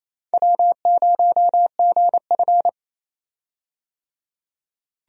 Morse code, heard as a keyed tone, about 28 wpm, 700 Hz. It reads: W0GF
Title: USA Call Signs - 4 Characters - 28wpm